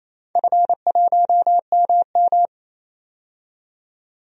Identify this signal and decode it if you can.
Morse code, heard as a keyed tone, about 28 wpm, 700 Hz. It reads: F1MM
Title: DXCC-100 Call Signs - 4 Characters - 28wpm